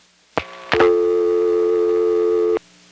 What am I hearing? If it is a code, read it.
Dial tone (North America), 350 + 440 Hz.
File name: phonetrips / EvanNX11HQ